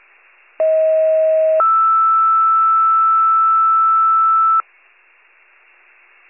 Two-tone page, heard as a one-second tone, then a three-second tone, about 640 then 1320 Hz. It is heard from 0.6 to 4.6 s.
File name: wcalert